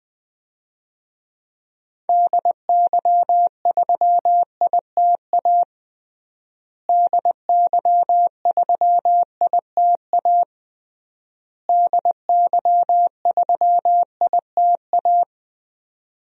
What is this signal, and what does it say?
Morse code, heard as a keyed tone, about 20 wpm, 700 Hz. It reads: DY3ITA DY3ITA DY3ITA
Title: DXCC-100 Call Signs - 6 Characters - Repeated 3x - 20wpm